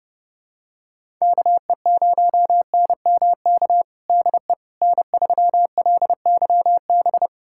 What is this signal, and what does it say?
Morse code, heard as a keyed tone, about 30 wpm, 700 Hz. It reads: KE0NMK DE N3LYB
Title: USA Callsign DE Callsign 30wpm